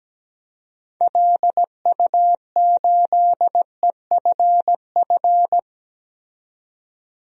Morse code, heard as a keyed tone, about 17 wpm, 700 Hz. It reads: LU8EFF
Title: DXCC-100 Call Signs - 6 Characters - 17wpm